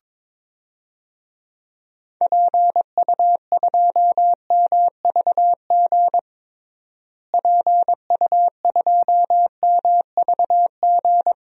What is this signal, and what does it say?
Morse code, heard as a keyed tone, about 22 wpm, 700 Hz. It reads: PU2MVG PU2MVG
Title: DXCC-100 Call Signs - 6 Characters - Repeated 2x - 22wpm